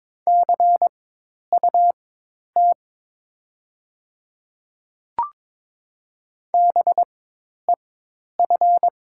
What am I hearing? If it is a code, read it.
Morse code, heard as a keyed tone, about 22 wpm, spaced as for 10 wpm, 700 Hz. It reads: CUT BEF
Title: 4x Character Spacing - Top 500 Words - 22wpm